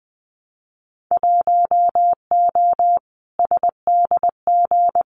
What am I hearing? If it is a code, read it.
Morse code, heard as a keyed tone, about 20 wpm, 700 Hz. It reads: 1O SDG
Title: Winter Field Day Exchanges Encore 20wpm